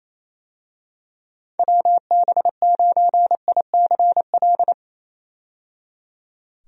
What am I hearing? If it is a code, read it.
Morse code, heard as a keyed tone, about 27 wpm, 700 Hz. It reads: WB9ICL
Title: USA Call Signs - 6 Characters - 28wpm